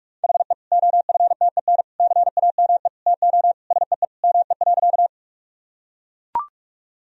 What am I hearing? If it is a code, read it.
Morse code, heard as a keyed tone, about 41 wpm, 700 Hz. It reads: HE OFTEN CAME TO SEE ME.
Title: Sentences from Top 1000 Words - Review Part 1 - 45wpm